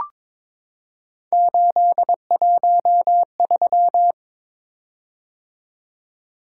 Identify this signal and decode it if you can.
Morse code, heard as a keyed tone, about 22 wpm, 700 Hz. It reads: 813
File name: practice / 22wpm / Three Letter-Number Combinations - 22wpm